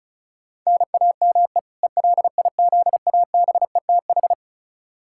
Morse code, heard as a keyed tone, about 36 wpm, 700 Hz. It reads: NAME ELIZABETH
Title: K1USN SST QSO Practice 35wpm